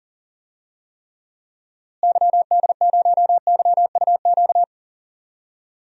Morse code, heard as a keyed tone, about 40 wpm, 700 Hz. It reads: YD0YUQ
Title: DXCC-100 Call Signs - 6 Characters - Repeated 2x - 40wpm